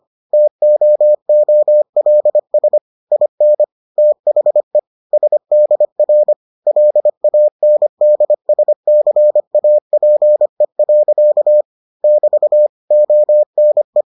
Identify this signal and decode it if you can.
Morse code, heard as a keyed tone, about 25 wpm, 600 Hz. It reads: TOOLS IN THE SDR LANDSCAPE. <BT> ONE